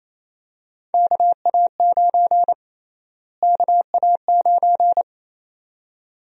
Morse code, heard as a keyed tone, about 28 wpm, 700 Hz. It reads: KA9 KA9
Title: USA Call Sign Prefixes - Repeated 2x - 28wpm